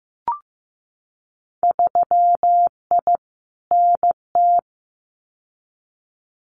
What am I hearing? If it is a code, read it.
Morse code, heard as a keyed tone, about 15 wpm, 700 Hz. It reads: 3I NT
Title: Winter Field Day Exchanges Encore 15wpm